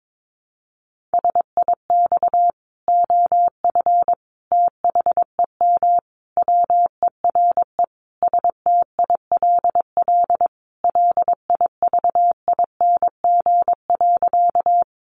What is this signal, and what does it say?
Morse code, heard as a keyed tone, about 22 wpm, 700 Hz. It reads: SIX OF THEM WERE STILL LIVING.
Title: Sentences from Top 1100 Words 22wpm